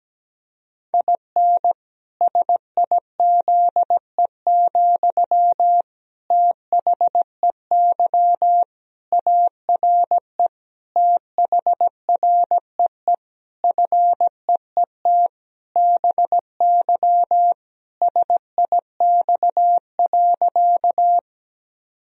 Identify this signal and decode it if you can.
Morse code, heard as a keyed tone, about 17 wpm, 700 Hz. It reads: IN SIZE, THEY ARE THREE FEET BY SIX.
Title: Sentences from Top 1000 Words - Review Part 2 - 17wpm